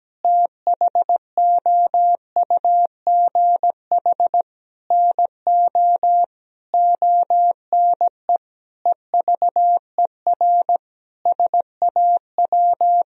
Morse code, heard as a keyed tone, about 17 wpm, 700 Hz. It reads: THOUGH NO ONE EVER SAW
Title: Sentences from Top 1000 Words - Review Part 2 - 17wpm